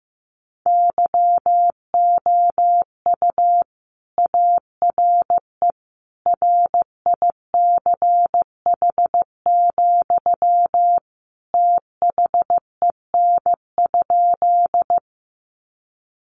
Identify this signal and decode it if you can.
Morse code, heard as a keyed tone, about 14 wpm, 700 Hz. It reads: YOU ARE RICH, THEN?
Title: Sentences from Top 1100 Words 15wpm